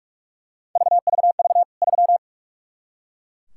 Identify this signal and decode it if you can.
Morse code, heard as a keyed tone, about 43 wpm, 700 Hz. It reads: VVV 3
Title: K1USN SST QSO Practice 45wpm